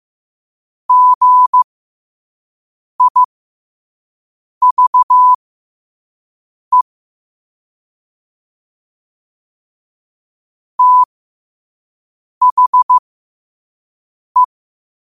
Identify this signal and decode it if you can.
Morse code, heard as a keyed tone, about 15 wpm, spaced as for 5 wpm, 1000 Hz. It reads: GIVE THE
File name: qotdfiles / 5wpm